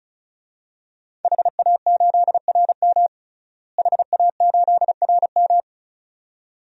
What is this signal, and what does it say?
Morse code, heard as a keyed tone, about 34 wpm, 700 Hz. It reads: HA8RM HA8RM
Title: DXCC-100 Call Signs - 5 Characters - Repeated 2x - 35wpm